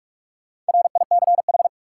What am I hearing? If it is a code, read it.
Morse code, heard as a keyed tone, about 46 wpm, 700 Hz. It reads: RICH
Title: Sentences from Top 1100 Words - New Word Drill - 45wpm